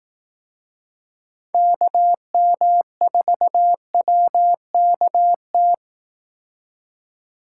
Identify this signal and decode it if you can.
Morse code, heard as a keyed tone, about 18 wpm, 700 Hz. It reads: KM4WKT
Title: Speed Racing - USA Call Signs 15wpm